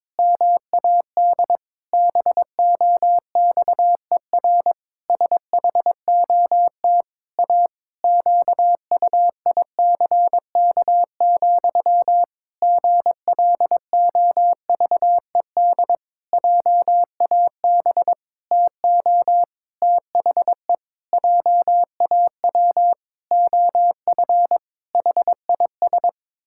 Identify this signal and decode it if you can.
Morse code, heard as a keyed tone, about 22 wpm, 700 Hz. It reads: MAD BOXER SHOT A QUICK, GLOVED JAB TO THE JAW OF HIS
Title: Pangram Challenge v2 - 22wpm